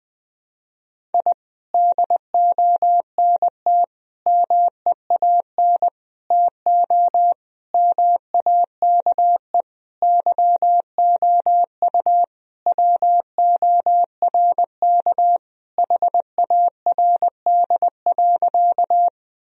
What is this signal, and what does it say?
Morse code, heard as a keyed tone, about 20 wpm, 700 Hz. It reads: I DONT MEAN TO MAKE YOU WORK HARD.
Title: Sentences from Top 1000 Words - Review Part 1 - 20wpm